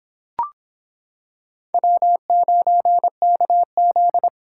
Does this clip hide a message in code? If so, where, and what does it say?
Yes: Morse code from 1.7 to 4.3 s, heard as a keyed tone, about 25 wpm, 700 Hz. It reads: W9KZ